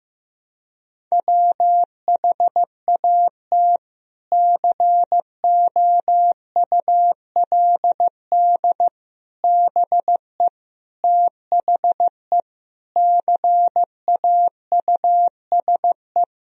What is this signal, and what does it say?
Morse code, heard as a keyed tone, about 14 wpm, 700 Hz. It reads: WHAT COULD BE THE CAUSE
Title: Sentences from Top 1000 Words - Review Part 1 - 15wpm